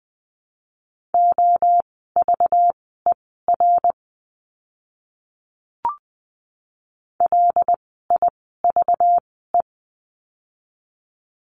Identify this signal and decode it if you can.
Morse code, heard as a keyed tone, about 20 wpm, spaced as for 14 wpm, 700 Hz. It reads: OVER LIVE
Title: 192 Lesson 31f - Letter v - Words 20wpm